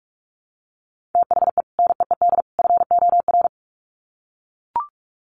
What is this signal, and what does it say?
Morse code, heard as a keyed tone, about 44 wpm, 700 Hz. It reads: THE NEED FOR
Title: Sets of 3 Words 45wpm